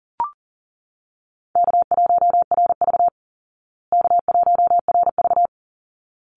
Morse code, heard as a keyed tone, about 40 wpm, 700 Hz. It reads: K1RV K1RV
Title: Easy Call Signs Repeated 2x - 40wpm